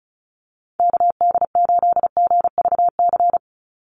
Morse code, heard as a keyed tone, about 34 wpm, 700 Hz. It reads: KD8GVC
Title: USA Call Signs 35wpm